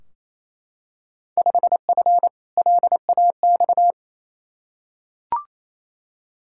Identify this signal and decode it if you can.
Morse code, heard as a keyed tone, about 28 wpm, 700 Hz. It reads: 5F LAX
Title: Field Day Exchanges Encore - 28wpm